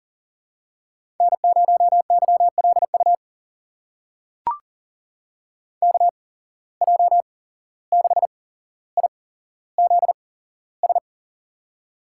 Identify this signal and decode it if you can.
Morse code, heard as a keyed tone, about 38 wpm, spaced as for 16 wpm, 700 Hz. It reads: N0YLU KJ6IZS
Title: USA Call Signs - 8x Character Spacing - 40wpm